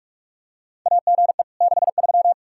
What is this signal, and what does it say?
Morse code, heard as a keyed tone, about 41 wpm, 700 Hz. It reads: AGE 63
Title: K1USN SST QSO Practice 45wpm